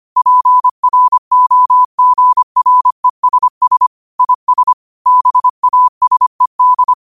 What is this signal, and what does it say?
Morse code, heard as a keyed tone, about 25 wpm, 1000 Hz. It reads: PROGRESS IS BASED
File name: qotdfiles / 25wpm